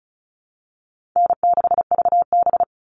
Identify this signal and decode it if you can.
Morse code, heard as a keyed tone, about 34 wpm, 700 Hz. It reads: N6VB
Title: All Cut Numbers - 4-Digit 35wpm